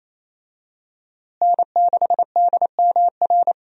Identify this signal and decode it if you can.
Morse code, heard as a keyed tone, about 28 wpm, 700 Hz. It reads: N6DMR
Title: USA Call Signs - 5 Characters - Repeated 3x - 28wpm